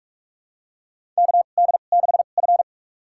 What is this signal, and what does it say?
Morse code, heard as a keyed tone, about 45 wpm, spaced as for 32 wpm, 700 Hz. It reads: KD6F
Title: USA Call Signs - 4 Characters - 2x Character Spacing - 45wpm